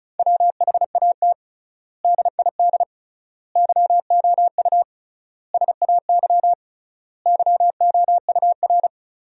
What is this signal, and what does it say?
Morse code, heard as a keyed tone, about 36 wpm, 700 Hz. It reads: WHAT DID YOU SAY YOUR
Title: 3x Word-Spacing - Sentences from Top 200 Words - 35wpm